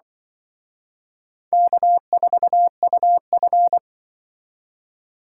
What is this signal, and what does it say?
Morse code, heard as a keyed tone, about 24 wpm, 700 Hz. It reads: K4UF
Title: Speed Racing - USA 1x2 Call Signs 20wpm